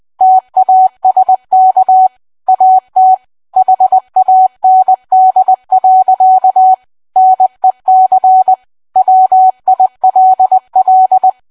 Morse code, heard as a keyed tone, about 20 wpm, 750 Hz. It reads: TASK AT HAND. NEC WILL